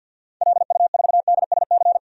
Morse code, heard as a keyed tone, about 49 wpm, 700 Hz. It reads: LU4DSC
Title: DXCC-100 Call Signs - 6 Characters - 50wpm